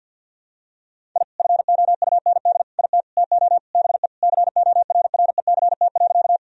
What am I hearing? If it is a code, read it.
Morse code, heard as a keyed tone, about 51 wpm, 700 Hz. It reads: I FOUND IT TO BE CORRECT.
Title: Sentences from Top 1000 Words - Review Part 2 - 50wpm